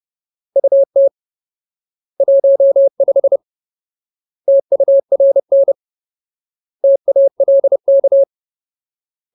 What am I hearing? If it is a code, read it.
Morse code, heard as a keyed tone, about 30 wpm, 550 Hz. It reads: UT 15 TURN TALK
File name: Ultimate Word List / 30WPM Ultimate